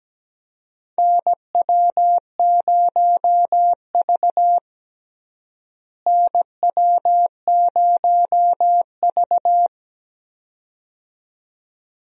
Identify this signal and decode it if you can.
Morse code, heard as a keyed tone, about 17 wpm, 700 Hz. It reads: NW0V NW0V
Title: USA Call Signs - 4 Characters - Repeated 2x - 17wpm